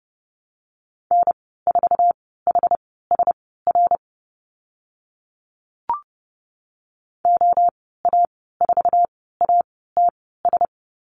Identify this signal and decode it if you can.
Morse code, heard as a keyed tone, about 30 wpm, spaced as for 17 wpm, 700 Hz. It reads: N4HSR OA4ATS
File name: Lesson 14g - Number 2 - Call Signs 30wpm